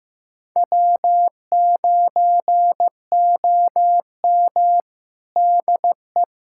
Morse code, heard as a keyed tone, about 15 wpm, 700 Hz. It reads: W9OM DE
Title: USA Callsign DE Callsign - 4 Characters - 15wpm